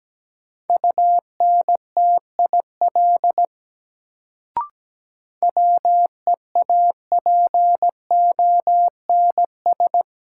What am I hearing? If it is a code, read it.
Morse code, heard as a keyed tone, about 17 wpm, 700 Hz. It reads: UNTIL WEAPONS
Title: Sets of 5 Words Encore - New Word Drill - 17wpm